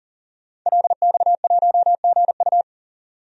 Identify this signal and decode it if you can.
Morse code, heard as a keyed tone, about 40 wpm, 700 Hz. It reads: LX1GU
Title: DXCC-100 Call Signs - 5 Characters - 40wpm